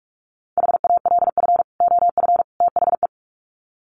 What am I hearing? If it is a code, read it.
Morse code, heard as a keyed tone, about 44 wpm, 700 Hz. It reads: HALF OF THE
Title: Sets of 3 Words 45wpm